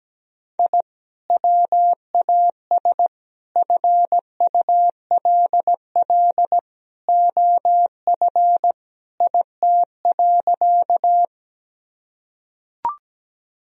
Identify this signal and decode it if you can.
Morse code, heard as a keyed tone, about 17 wpm, 700 Hz. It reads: I WAS FULL OF IT.
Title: Sentences from Top 1000 Words - Review Part 3 - 17wpm